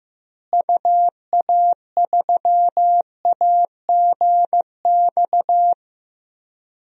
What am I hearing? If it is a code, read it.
Morse code, heard as a keyed tone, about 15 wpm, 700 Hz. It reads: UA3AGX
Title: DXCC-100 Call Signs - 6 Characters - Repeated 3x - 15wpm